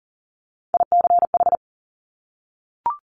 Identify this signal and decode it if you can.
Morse code, heard as a keyed tone, about 37 wpm, 700 Hz. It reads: ICH
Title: Top Three Letter Combinations in Words 40wpm